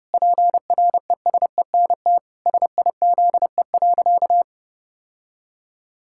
Morse code, heard as a keyed tone, about 30 wpm, 700 Hz. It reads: PRESENT SIZE.
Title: Sentences from Top 1000 Words - Review Part 6 - 30wpm